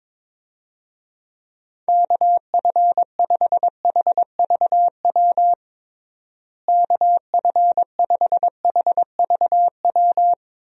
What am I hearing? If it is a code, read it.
Morse code, heard as a keyed tone, about 22 wpm, 700 Hz. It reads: KF5HVW KF5HVW
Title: USA Call Signs - 6 Characters - Repeated 3x - 22wpm